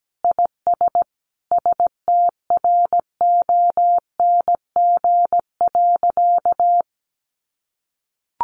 Morse code, heard as a keyed tone, about 17 wpm, 700 Hz. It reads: IS STRONG.
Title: Sentences from Top 1400 Words 17wpm